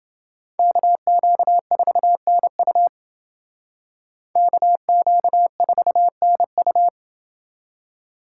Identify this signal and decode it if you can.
Morse code, heard as a keyed tone, about 29 wpm, 700 Hz. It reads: KQ4NU KQ4NU